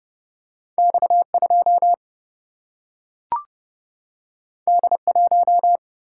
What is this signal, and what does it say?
Morse code, heard as a keyed tone, about 30 wpm, 700 Hz. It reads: X 2 D 1
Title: Single-Letter Single-Number 30wpm